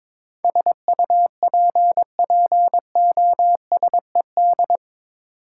Morse code, heard as a keyed tone, about 22 wpm, 700 Hz. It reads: SUPPOSED
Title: Sets of 3 Words - New Word Drill - 22wpm